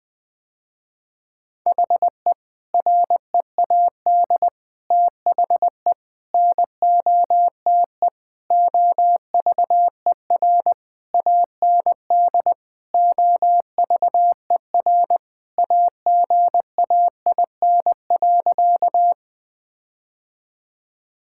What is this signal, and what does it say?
Morse code, heard as a keyed tone, about 20 wpm, 700 Hz. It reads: HE READ THE NOTE OVER AND OVER AGAIN.
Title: Sentences with 4-Word Phrases - Spoken Beginning - 20wpm